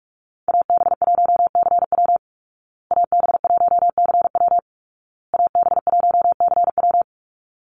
Morse code, heard as a keyed tone, about 44 wpm, 700 Hz. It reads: AB1CW AB1CW AB1CW
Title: Call Signs Repeated 2x - 45wpm